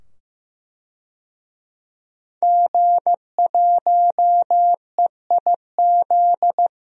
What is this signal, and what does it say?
Morse code, heard as a keyed tone, about 16 wpm, 700 Hz. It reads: G1EIZ
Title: DXCC-100 Call Signs - 5 Characters - 15wpm